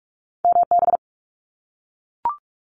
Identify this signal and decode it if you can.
Morse code, heard as a keyed tone, about 44 wpm, 700 Hz. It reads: MB
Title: Canadian Provinces and Territories - Repeated 3x - 45wpm